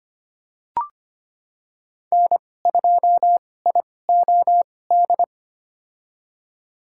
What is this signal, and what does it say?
Morse code, heard as a keyed tone, about 25 wpm, spaced as for 18 wpm, 700 Hz. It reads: N2IOD
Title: USA Call Signs - 2x Character Spacing - 25wpm